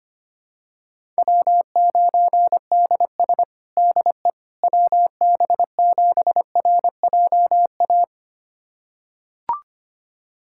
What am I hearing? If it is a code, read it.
Morse code, heard as a keyed tone, about 25 wpm, 700 Hz. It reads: W9DS DE WB7RJA
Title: USA Callsign DE Callsign 25wpm